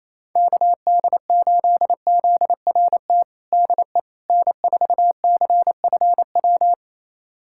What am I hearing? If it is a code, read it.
Morse code, heard as a keyed tone, about 28 wpm, 700 Hz. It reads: KD8ZRT DE N4CFW
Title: USA Callsign DE Callsign 28wpm